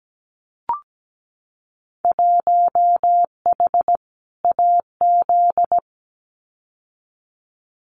Morse code, heard as a keyed tone, about 17 wpm, 700 Hz. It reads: 1H AZ
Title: Winter Field Day Exchanges 17wpm